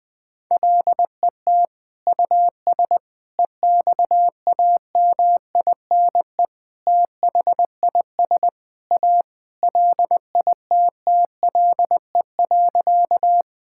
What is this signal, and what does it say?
Morse code, heard as a keyed tone, about 20 wpm, 700 Hz. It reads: LET US EXAMINE THIS A LITTLE.
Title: Sentences from Top 1600 Words 20wpm